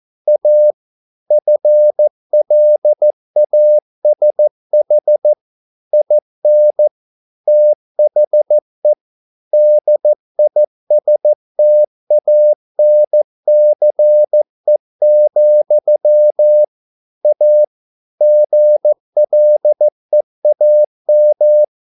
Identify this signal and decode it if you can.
Morse code, heard as a keyed tone, about 15 wpm, 600 Hz. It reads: A FLASH IN THE DISTANCE, A GLEAM